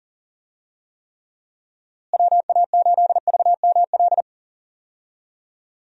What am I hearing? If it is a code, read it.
Morse code, heard as a keyed tone, about 40 wpm, 700 Hz. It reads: WA8VML
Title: USA Call Signs - 6 Characters - Repeated 3x - 40wpm